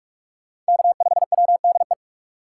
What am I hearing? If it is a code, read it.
Morse code, heard as a keyed tone, about 58 wpm, 700 Hz. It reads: K5WDE